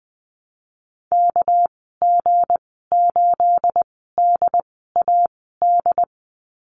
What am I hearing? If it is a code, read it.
Morse code, heard as a keyed tone, about 20 wpm, spaced as for 15 wpm, 700 Hz. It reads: KG8DAD
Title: 219 Lesson 35g - Number 0 - Call Signs 20wpm